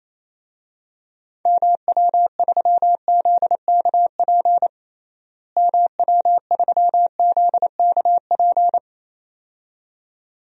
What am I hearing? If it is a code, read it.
Morse code, heard as a keyed tone, about 27 wpm, 700 Hz. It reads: MW3ZKP MW3ZKP
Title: DXCC-100 Call Signs - 6 Characters - Repeated 2x - 28wpm